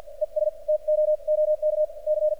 Morse code, heard as a keyed tone, about 46 wpm, spaced as for 39 wpm, 610 Hz. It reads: ES TOOK C
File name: archived / CRSnet